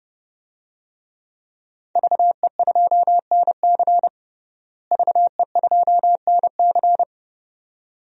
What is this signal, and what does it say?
Morse code, heard as a keyed tone, about 30 wpm, 700 Hz. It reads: VE2NC VE2NC
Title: DXCC-100 Call Signs - 5 Characters - Repeated 2x - 30wpm